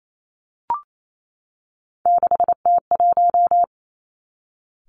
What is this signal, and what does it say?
Morse code, heard as a keyed tone, about 28 wpm, 700 Hz. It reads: 6T1
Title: Three Challenging Letter-Number Combinations 28wpm